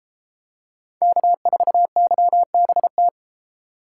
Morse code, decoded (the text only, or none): K4YBT